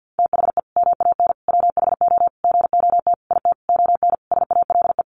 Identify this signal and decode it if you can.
Morse code, heard as a keyed tone, about 50 wpm, 700 Hz. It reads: THE MAN WHO GOT IT ON SALE
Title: Sentences from Top 1400 Words 50wpm